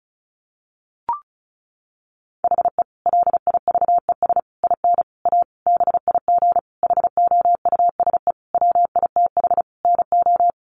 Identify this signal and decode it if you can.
Morse code, heard as a keyed tone, about 34 wpm, 700 Hz. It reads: HE LIVES IN A BIG HOUSE WITH NO
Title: Additional Easy Phrases 35wpm